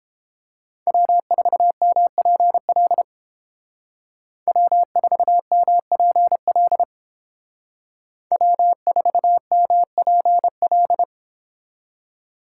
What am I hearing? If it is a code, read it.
Morse code, heard as a keyed tone, about 29 wpm, 700 Hz. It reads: W4MPL W4MPL W4MPL